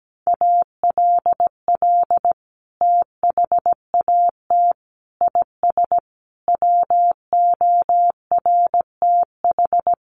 Morse code, heard as a keyed tone, about 17 wpm, 700 Hz. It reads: ALL THAT IS WORTH
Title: Sentences from Top 1100 Words 17wpm